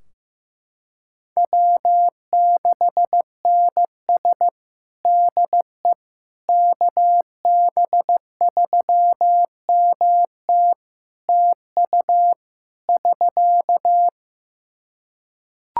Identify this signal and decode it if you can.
Morse code, heard as a keyed tone, about 15 wpm, 700 Hz. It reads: W6NS DE KB3MT TU <SK>
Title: Practice QSO - Exchange 4 - Ending - 15wpm